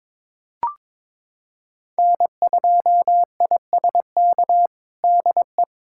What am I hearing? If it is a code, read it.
Morse code, heard as a keyed tone, about 22 wpm, 700 Hz. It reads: N2ISK DE
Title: USA Callsign DE Callsign 22wpm